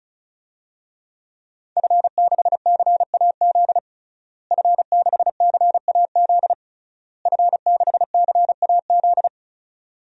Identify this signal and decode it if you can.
Morse code, heard as a keyed tone, about 34 wpm, 700 Hz. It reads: F6CAZ F6CAZ F6CAZ
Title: DXCC-100 Call Signs - 5 Characters - Repeated 3x - 35wpm